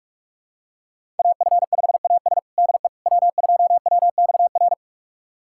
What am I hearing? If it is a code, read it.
Morse code, heard as a keyed tone, about 46 wpm, 700 Hz. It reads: AF5AS DE W2WXR
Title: USA Callsign DE Callsign - 5 Characters - 45wpm